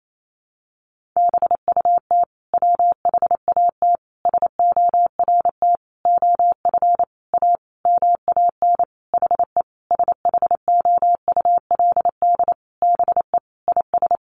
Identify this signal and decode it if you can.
Morse code, heard as a keyed tone, about 28 wpm, 700 Hz. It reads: BUT WHAT SORT OF A MAN HE SHOULD BE IS
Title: Sentences from Top 1300 Words 28wpm